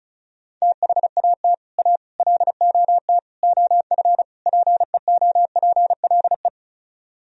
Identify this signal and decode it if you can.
Morse code, heard as a keyed tone, about 45 wpm, 700 Hz. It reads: THAT A LOT OF PEOPLE